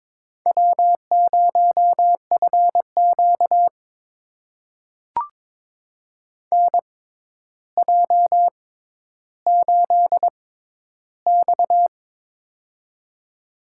Morse code, heard as a keyed tone, about 22 wpm, spaced as for 11 wpm, 700 Hz. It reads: W0FQ NJ8X
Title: USA Call Signs - 4 Characters - 6x Character Spacing - 22wpm